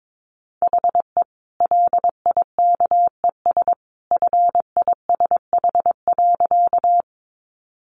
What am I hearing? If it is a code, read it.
Morse code, heard as a keyed tone, about 22 wpm, 700 Hz. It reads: HE LIKES FISH.
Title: Additional Easy Phrases 22wpm